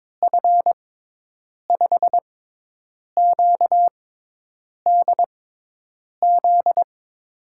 Morse code, heard as a keyed tone, about 22 wpm, spaced as for 8 wpm, 700 Hz. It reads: F5QDZ
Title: USA Call Signs - 6 Characters - 6x Character Spacing - 22wpm